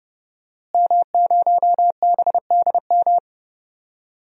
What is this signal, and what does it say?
Morse code, heard as a keyed tone, about 31 wpm, 700 Hz. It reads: M0BDM
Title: DXCC-100 Call Signs - 5 Characters - Repeated 3x - 30wpm